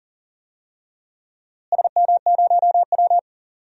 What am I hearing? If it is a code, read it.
Morse code, heard as a keyed tone, about 38 wpm, 700 Hz. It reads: SM0W
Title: DXCC-100 Call Signs - 4 Characters - Repeated 3x - 40wpm